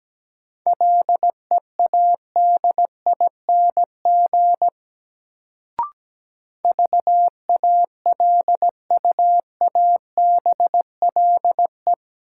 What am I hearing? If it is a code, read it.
Morse code, heard as a keyed tone, about 17 wpm, 700 Hz. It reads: LEADING VALUABLE
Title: Sentences from Top 1100 Words - New Word Drill Alt - 17wpm